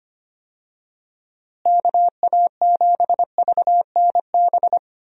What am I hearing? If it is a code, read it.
Morse code, heard as a keyed tone, about 25 wpm, 700 Hz. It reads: KA7VNB
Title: USA Call Signs - 6 Characters - Repeated 3x - 25wpm